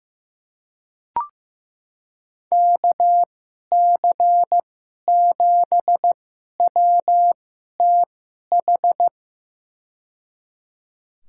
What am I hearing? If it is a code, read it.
Morse code, heard as a keyed tone, about 15 wpm, spaced as for 11 wpm, 700 Hz. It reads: KC7WTH
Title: USA Call Signs - 6 Characters - 2x Character Spacing - 15wpm